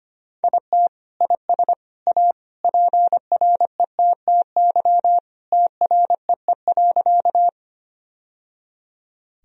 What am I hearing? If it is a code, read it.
Morse code, heard as a keyed tone, about 25 wpm, 700 Hz. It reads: IT IS A PRETTY TREE.
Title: Sentences from Top 1000 Words - Review Part 1 - 25wpm